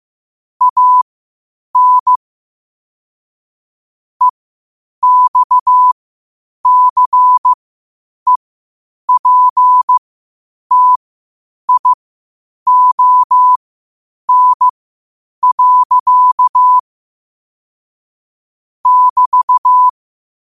Morse code, heard as a keyed tone, about 15 wpm, spaced as for 8 wpm, 1000 Hz. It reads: AN EXCEPTION. <BT>
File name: qotdfiles / 8wpm